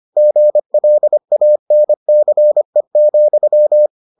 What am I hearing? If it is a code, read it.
Morse code, heard as a keyed tone, about 25 wpm, 600 Hz. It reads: GLANCE,